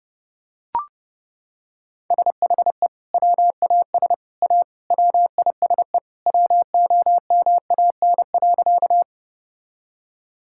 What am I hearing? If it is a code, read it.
Morse code, heard as a keyed tone, about 30 wpm, 700 Hz. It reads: SHE WAS A WISE WOMAN.
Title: Sentences from Top 1600 Words 30wpm